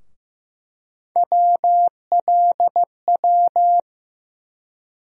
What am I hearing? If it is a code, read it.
Morse code, heard as a keyed tone, about 15 wpm, 700 Hz. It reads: WLW
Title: USA Call Sign Postfixes - 3 Letters 15wpm